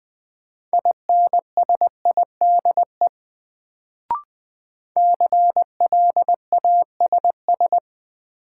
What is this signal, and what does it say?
Morse code, heard as a keyed tone, about 21 wpm, 700 Hz. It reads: INSIDE CLASS
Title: Sentences from Top 1000 Words - Review Part 2 - New Word Drill - 20wpm